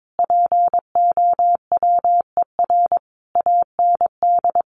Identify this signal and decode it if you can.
Morse code, heard as a keyed tone, about 22 wpm, 700 Hz. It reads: POWER AND
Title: Sentences from Top 1200 Words 22wpm